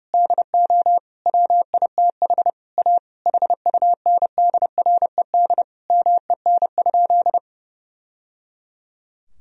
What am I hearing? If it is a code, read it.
Morse code, heard as a keyed tone, about 30 wpm, 700 Hz. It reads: DO WITH A HUNDRED MEN?
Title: Sentences from Top 1000 Words - Review Part 2 - 30wpm